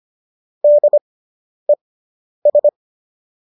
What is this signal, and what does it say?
Morse code, heard as a keyed tone, about 25 wpm, spaced as for 10 wpm, 590 Hz. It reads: DES